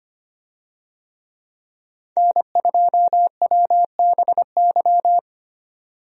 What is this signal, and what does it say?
Morse code, heard as a keyed tone, about 25 wpm, 700 Hz. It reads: N2WBY
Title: USA Call Signs - 5 Characters - 25wpm